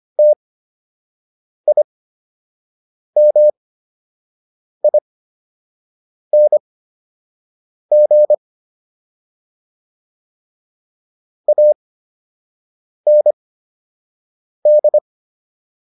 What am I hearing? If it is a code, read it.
Morse code, heard as a keyed tone, about 25 wpm, spaced as for 6 wpm, 600 Hz. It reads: TIMING AND